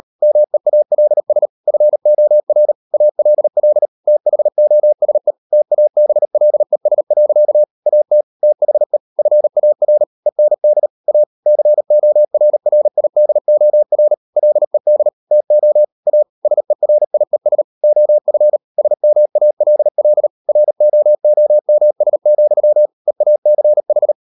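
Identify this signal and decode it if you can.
Morse code, heard as a keyed tone, about 38 wpm, 600 Hz. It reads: MEALS FOR ALL THOSE TABLES. AT THE FAR END A CORRIDOR LED TO A SERIES OF SMALL ROOMS, EACH
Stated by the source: Morse code practice at 38 WPM, sending Ken Follett's novel Never in English, from CW med Gnister.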